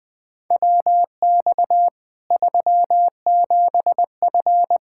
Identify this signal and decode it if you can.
Morse code, heard as a keyed tone, about 20 wpm, 700 Hz. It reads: WX 37F
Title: Practice QSO - Exchange 2 - Info - 20wpm